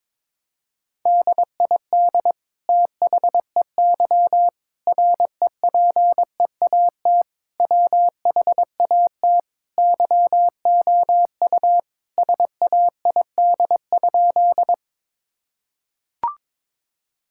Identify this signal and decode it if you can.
Morse code, heard as a keyed tone, about 22 wpm, 700 Hz. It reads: DID THEY REPEAT WHAT YOU SAID?
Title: Sentences from Top 1600 Words 22wpm